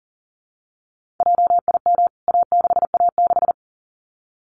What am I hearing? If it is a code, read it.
Morse code, heard as a keyed tone, about 40 wpm, 700 Hz. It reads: JIM A6A6
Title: CWT Exchange 40wpm